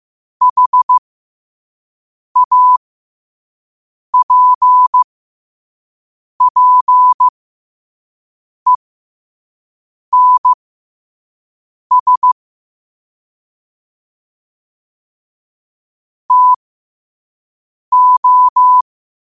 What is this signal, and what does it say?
Morse code, heard as a keyed tone, about 15 wpm, spaced as for 5 wpm, 1000 Hz. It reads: HAPPENS TO
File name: qotdfiles / 5wpm